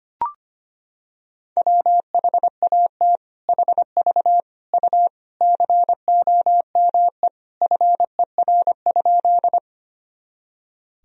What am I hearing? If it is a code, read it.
Morse code, heard as a keyed tone, about 25 wpm, 700 Hz. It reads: WHAT HV U COME FER?
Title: Sentences from Top 100 Words and Abbreviations 25wpm